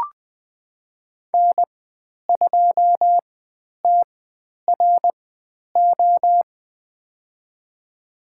Morse code, heard as a keyed tone, about 20 wpm, spaced as for 10 wpm, 700 Hz. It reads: N2TRO
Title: 095 Lesson 16g - Letter c - Call Signs 20wpm